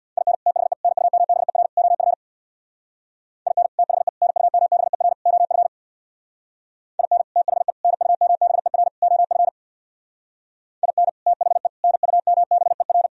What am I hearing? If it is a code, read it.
Morse code, heard as a keyed tone, about 64 wpm, 700 Hz. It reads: IN THE NUMBER OF IN THE NUMBER OF IN THE NUMBER OF IN THE NUMBER